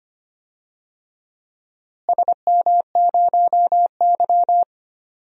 Morse code, heard as a keyed tone, about 25 wpm, 700 Hz. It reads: SM0Y
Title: DXCC-100 Call Signs - 4 Characters - Repeated 2x - 25wpm